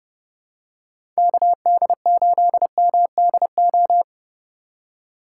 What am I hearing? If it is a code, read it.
Morse code, heard as a keyed tone, about 30 wpm, 700 Hz. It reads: KD8MDO
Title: USA Call Signs - 3x Character Spacing - 30wpm